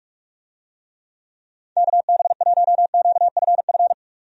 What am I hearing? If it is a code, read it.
Morse code, heard as a keyed tone, about 47 wpm, 700 Hz. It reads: KB1QFF
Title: USA Call Signs - 6 Characters - 45wpm